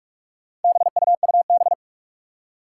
Morse code, heard as a keyed tone, about 46 wpm, 700 Hz. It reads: BUUB
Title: Four Challenging Letter-Number Combinations 45wpm